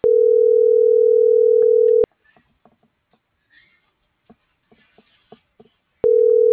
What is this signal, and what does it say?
Ringback tone (North America), 440 + 480 Hz.